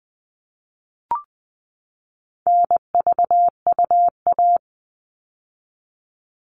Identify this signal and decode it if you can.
Morse code, heard as a keyed tone, about 20 wpm, 700 Hz. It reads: NVUA
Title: All Cut Numbers - 4-Digit 20wpm